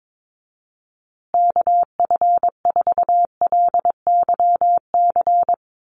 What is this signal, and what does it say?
Morse code, heard as a keyed tone, about 22 wpm, 700 Hz. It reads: KF4LYC
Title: USA Call Signs Repeated 2x 22wpm